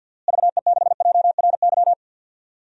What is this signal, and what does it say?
Morse code, heard as a keyed tone, about 51 wpm, 700 Hz. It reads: VE6JRY
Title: DXCC-100 Call Signs - 6 Characters - Repeated 3x - 50wpm